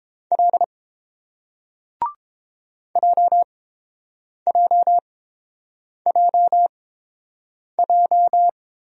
Morse code, heard as a keyed tone, about 30 wpm, spaced as for 7 wpm, 700 Hz. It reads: L JJJJ